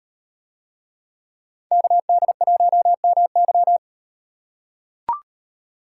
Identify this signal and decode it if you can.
Morse code, heard as a keyed tone, about 36 wpm, 700 Hz. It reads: KD1MY